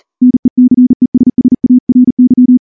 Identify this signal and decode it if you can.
Morse code, heard as a keyed tone, about 38 wpm, 250 Hz. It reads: NECESSARY
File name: morse / morse-training-40WPM